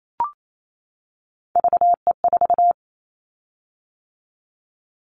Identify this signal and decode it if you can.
Morse code, heard as a keyed tone, about 28 wpm, 700 Hz. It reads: VE4
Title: Callsign Prefixes for Top 25 DXCC - 28wpm